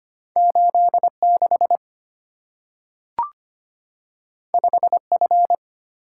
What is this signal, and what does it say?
Morse code, heard as a keyed tone, about 25 wpm, 690 Hz. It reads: 86 5F
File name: Two Letter-Number Combinations - 25wpm